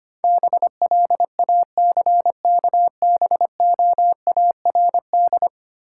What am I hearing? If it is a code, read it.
Morse code, heard as a keyed tone, about 25 wpm, 700 Hz. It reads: BLACKBOARD
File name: practice / 25wpm / Easy Words with Hints - 25wpm